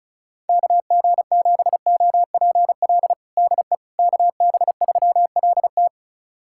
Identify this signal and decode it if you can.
Morse code, heard as a keyed tone, about 34 wpm, 700 Hz. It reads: KG7OPL DE KB3LT
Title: USA Callsign DE Callsign 35wpm